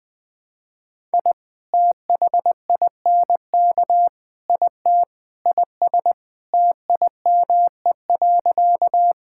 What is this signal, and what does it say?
Morse code, heard as a keyed tone, about 20 wpm, 700 Hz. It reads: I THINK IT IS TIME.
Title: Sentences with 4-Word Phrases - 20wpm